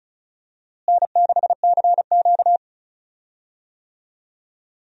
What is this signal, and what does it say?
Morse code, heard as a keyed tone, about 36 wpm, 700 Hz. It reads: N6CQ
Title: USA Call Signs - 4 Characters - Repeated 2x - 35wpm